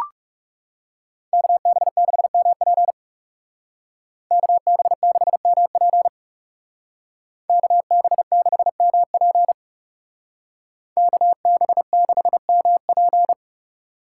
Morse code, heard as a keyed tone, about 39 wpm, 700 Hz. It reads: KB6MP KB6MP KB6MP KB6MP